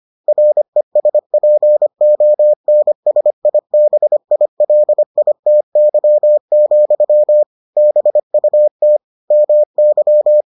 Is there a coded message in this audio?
Yes: Morse code from 0.3 to 10.4 s, heard as a keyed tone, about 25 wpm, 600 Hz. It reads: RESPONSIBILITY, BUT MY